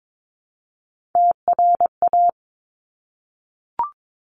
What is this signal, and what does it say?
Morse code, heard as a keyed tone, about 22 wpm, 700 Hz. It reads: TRA
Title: Top Three Letter Combinations in Words 22wpm